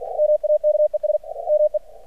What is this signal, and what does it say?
Morse code, heard as a keyed tone, about 50 wpm, 600 Hz, with radio noise. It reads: MAKES ME
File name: archived / CRSnet